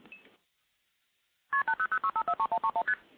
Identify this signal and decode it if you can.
Multi-frequency trunk signalling, heard as fast pairs of tones, tones of about 90 ms. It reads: KP8006543131ST